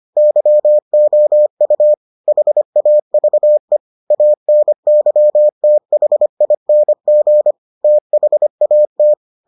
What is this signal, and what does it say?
Morse code, heard as a keyed tone, about 25 wpm, 600 Hz. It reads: YOU HAVE ANYTHING THAT